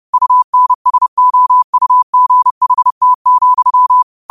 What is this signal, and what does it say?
Morse code, heard as a keyed tone, about 30 wpm, 1000 Hz. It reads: UNSOUGHT,
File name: qotdfiles / 30wpm